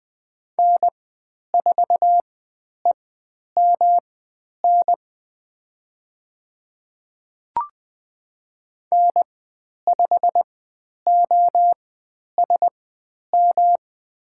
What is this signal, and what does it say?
Morse code, heard as a keyed tone, about 20 wpm, spaced as for 10 wpm, 700 Hz. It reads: N4EMN N5OSM
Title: 109 Lesson 18g - Letter m - Call Signs 20wpm